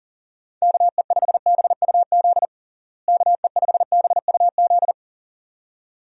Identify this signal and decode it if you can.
Morse code, heard as a keyed tone, about 40 wpm, 700 Hz. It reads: KE5BUZ KE5BUZ
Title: USA Call Signs - 6 Characters - Repeated 2x - 40wpm